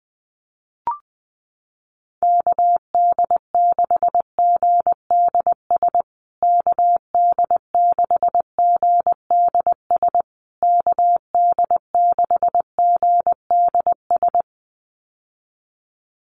Morse code, heard as a keyed tone, about 20 wpm, 700 Hz. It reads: KD6GDS KD6GDS KD6GDS
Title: USA Call Signs Repeated 3x 20wpm